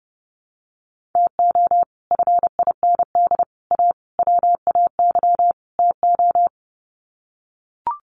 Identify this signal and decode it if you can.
Morse code, heard as a keyed tone, about 30 wpm, 700 Hz. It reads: TO FIND A WAY TO
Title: Sets of 5 Words 30wpm